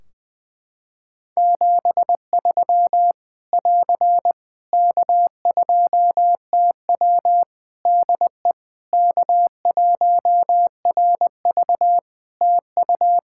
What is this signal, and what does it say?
Morse code, heard as a keyed tone, about 20 wpm, 700 Hz. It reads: 73 <AR> K2TW DE K1RV TU
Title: K1USN SST QSO Practice 20wpm